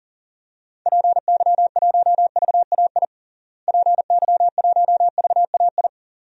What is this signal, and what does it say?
Morse code, heard as a keyed tone, about 40 wpm, 700 Hz. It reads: PY1VAI PY1VAI
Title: DXCC-100 Call Signs - 6 Characters - Repeated 2x - 40wpm